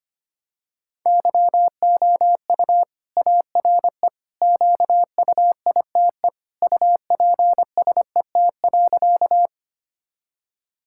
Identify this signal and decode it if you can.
Morse code, heard as a keyed tone, about 25 wpm, 700 Hz. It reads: YOU ARE QUITE UPSET.
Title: Sentences from Top 1000 Words - Review Part 5 - 25wpm